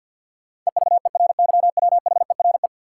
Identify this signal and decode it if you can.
Morse code, heard as a keyed tone, about 50 wpm, 700 Hz. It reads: EVERYWHERE
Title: Sentences from Top 1600 Words - New Word Drill - 50wpm